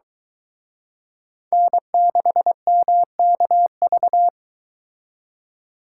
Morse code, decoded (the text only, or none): N6MKV